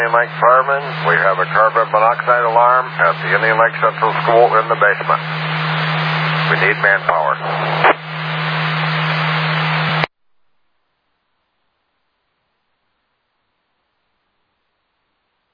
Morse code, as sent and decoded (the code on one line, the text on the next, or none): none